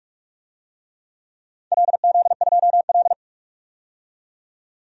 Morse code, heard as a keyed tone, about 41 wpm, 700 Hz. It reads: LZ2L
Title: DXCC-100 Call Signs - 4 Characters - 45wpm